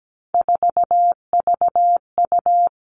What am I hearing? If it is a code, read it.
Morse code, heard as a keyed tone, about 17 wpm, 700 Hz. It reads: 4VU
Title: Three Challenging Letter-Number Combinations 17wpm